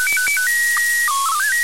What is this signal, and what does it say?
SSTV (slow-scan television), PD 120.
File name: ISS-SSTV-audio 145802924Hz